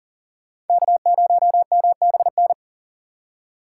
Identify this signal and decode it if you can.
Morse code, heard as a keyed tone, about 51 wpm, 700 Hz. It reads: K0MBN